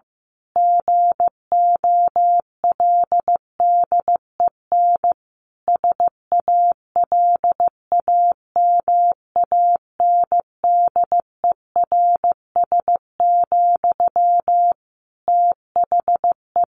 Morse code, heard as a keyed tone, about 15 wpm, 700 Hz. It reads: GOLDEN SALAMANDERS, THE
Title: A Sound Of Thunder 15wpm